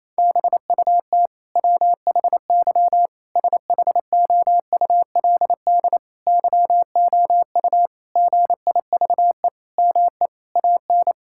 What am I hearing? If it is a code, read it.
Morse code, heard as a keyed tone, about 27 wpm, 700 Hz. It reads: BUT WHY SHOULD YOU GIVE ME AN
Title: Sentences from Top 1000 Words - Review Part 6 - 28wpm